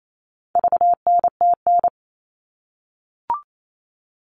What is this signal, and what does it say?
Morse code, heard as a keyed tone, about 28 wpm, 700 Hz. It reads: VNTN
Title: All Cut Numbers - 4-Digit 28wpm